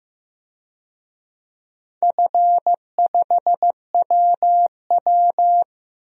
Morse code, heard as a keyed tone, about 15 wpm, 700 Hz. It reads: F5WW
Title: DXCC-100 Call Signs - 4 Characters - Repeated 3x - 15wpm